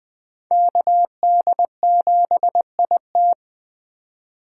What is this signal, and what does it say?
Morse code, heard as a keyed tone, about 20 wpm, 700 Hz. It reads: KD7IT
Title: USA Call Signs - 5 Characters - 4x Character Spacing - 20wpm